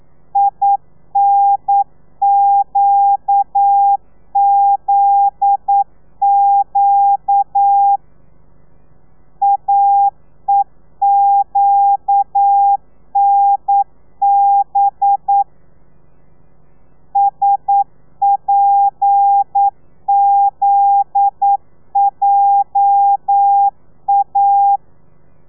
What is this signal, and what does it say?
Morse code, heard as a keyed tone, about 9 wpm, 800 Hz. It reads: INQZQ AEQNB SPZJA
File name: telegrafia / i2tfj / Lez9 40